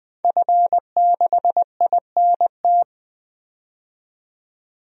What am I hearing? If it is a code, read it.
Morse code, heard as a keyed tone, about 20 wpm, 700 Hz. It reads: F6INT
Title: DXCC-100 Call Signs - 5 Characters - 20wpm